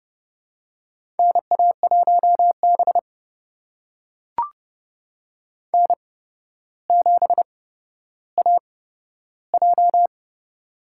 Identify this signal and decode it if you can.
Morse code, heard as a keyed tone, about 30 wpm, spaced as for 13 wpm, 700 Hz. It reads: NA1B N7AJ
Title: USA Call Signs - 4 Characters - 8x Character Spacing - 30wpm